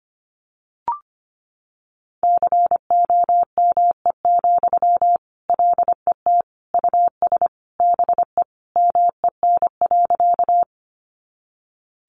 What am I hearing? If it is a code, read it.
Morse code, heard as a keyed tone, about 25 wpm, 700 Hz. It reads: COME, LET US BE MEN.
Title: Sentences from Top 300 Words 25wpm